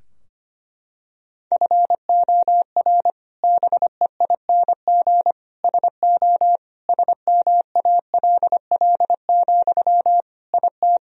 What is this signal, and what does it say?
Morse code, heard as a keyed tone, about 25 wpm, 700 Hz. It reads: FOR BEING SO SMALL, IT
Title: Sentences from Top 1000 Words - Review Part 3 - 25wpm